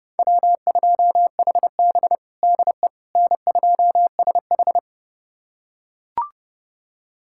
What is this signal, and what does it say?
Morse code, heard as a keyed tone, about 30 wpm, 700 Hz. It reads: W2HB DE N2SH
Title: USA Callsign DE Callsign - 4 Characters - 30wpm